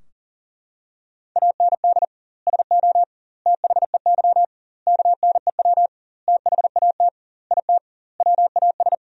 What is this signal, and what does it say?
Morse code, heard as a keyed tone, about 37 wpm, 700 Hz. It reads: AND SO THEY KNEW THAT IT WAS
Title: 2x Word-Spacing - Sentences from Top 400 Words - 40wpm